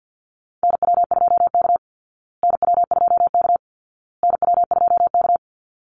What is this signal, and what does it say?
Morse code, heard as a keyed tone, about 48 wpm, 700 Hz. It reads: NW2K NW2K NW2K
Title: Call Signs Repeated 2x - 50wpm